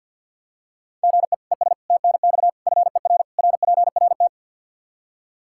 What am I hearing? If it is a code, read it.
Morse code, heard as a keyed tone, about 51 wpm, 700 Hz. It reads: GE ES TNX FER RPRT
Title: Practice QSO - Exchange 1 - Beginning - 50wpm